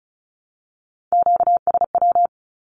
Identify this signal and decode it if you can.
Morse code, heard as a keyed tone, about 34 wpm, 700 Hz. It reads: QSW
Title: Q-Codes 35wpm